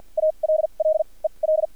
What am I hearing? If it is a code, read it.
Morse code, heard as a keyed tone, about 50 wpm, spaced as for 36 wpm, 630 Hz. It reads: ALL EL
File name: archived / CRSnet